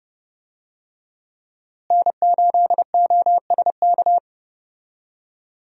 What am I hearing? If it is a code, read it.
Morse code, heard as a keyed tone, about 32 wpm, 700 Hz. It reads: N8OSK
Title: USA Call Signs - 5 Characters - Repeated 2x - 30wpm